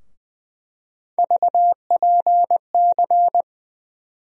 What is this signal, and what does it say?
Morse code, heard as a keyed tone, about 20 wpm, 700 Hz. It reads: VPC
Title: USA Call Sign Postfixes - 3 Letters 20wpm